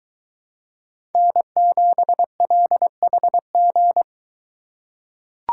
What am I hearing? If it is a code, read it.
Morse code, heard as a keyed tone, about 21 wpm, 700 Hz. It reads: N7LHG